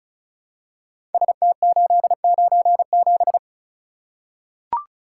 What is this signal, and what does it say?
Morse code, heard as a keyed tone, about 36 wpm, 700 Hz. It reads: ST897
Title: Code Groups - 5 Letter:Numbers - 35wpm